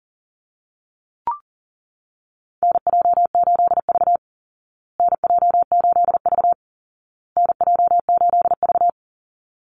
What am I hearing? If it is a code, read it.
Morse code, heard as a keyed tone, about 40 wpm, 700 Hz. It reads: NJ8V NJ8V NJ8V
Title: Call Signs Repeated 2x - 40wpm